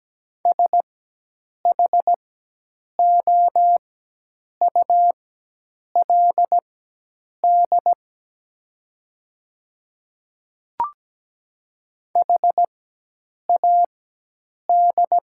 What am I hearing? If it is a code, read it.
Morse code, heard as a keyed tone, about 17 wpm, spaced as for 8 wpm, 700 Hz. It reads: SHOULD HAD
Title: 4x Character Spacing - Top 500 Words - 17wpm